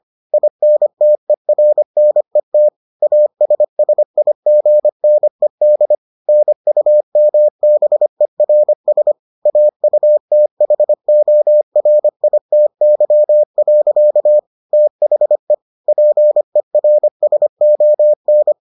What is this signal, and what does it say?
Morse code, heard as a keyed tone, about 25 wpm, 600 Hz. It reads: INTERNET ASSIGNED NUMBERS AUTHORITY. THE PERSON